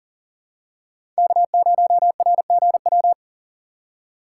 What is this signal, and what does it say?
Morse code, heard as a keyed tone, about 40 wpm, 700 Hz. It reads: K0RGW
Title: USA Call Signs - 5 Characters - 2x Character Spacing - 40wpm